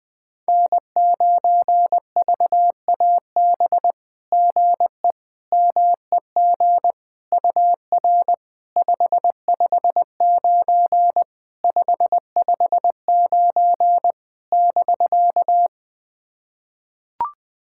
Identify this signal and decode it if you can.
Morse code, heard as a keyed tone, about 20 wpm, 700 Hz. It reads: N9VAB GE MEG UR 559 559 <BK>
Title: SOTA QSO Practice 20wpm